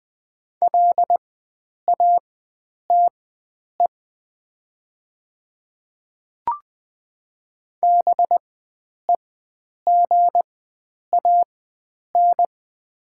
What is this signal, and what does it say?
Morse code, heard as a keyed tone, about 20 wpm, spaced as for 9 wpm, 700 Hz. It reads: LATE BEGAN
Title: 4x Character Spacing - Top 300 Words - 20wpm